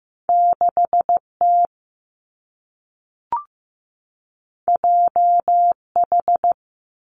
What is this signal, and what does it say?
Morse code, heard as a keyed tone, about 15 wpm, 700 Hz. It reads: 6 T J H
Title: Two Challenging Letter-Number Combinations 15wpm